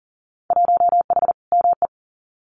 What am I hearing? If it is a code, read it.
Morse code, heard as a keyed tone, about 40 wpm, 700 Hz. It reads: 1H ME
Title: Winter Field Day Exchanges 40wpm